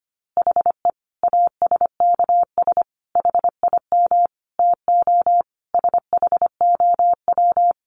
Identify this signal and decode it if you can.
Morse code, heard as a keyed tone, about 25 wpm, 700 Hz. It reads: HE ASKS HIM TO SHOW
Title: Easy Phrases 25wpm